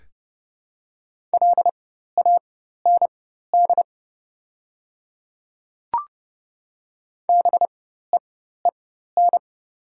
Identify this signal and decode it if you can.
Morse code, heard as a keyed tone, about 29 wpm, spaced as for 14 wpm, 700 Hz. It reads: LAND BEEN
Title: 4x Character Spacing - Top 200 Words - 30wpm